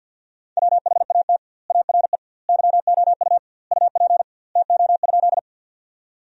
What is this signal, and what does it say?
Morse code, heard as a keyed tone, about 50 wpm, 700 Hz. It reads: WHAT ARE YOU UP TO?
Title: 2x Word-Spacing - Sentences from Top 100 Words - 50wpm